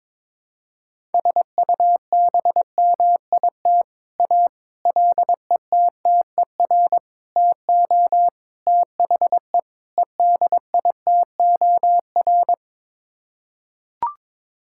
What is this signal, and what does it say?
Morse code, heard as a keyed tone, about 21 wpm, 700 Hz. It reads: SUBMIT A LETTER TO THE EDITOR
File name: practice / 22wpm / Sets of 6 Words - 22wpm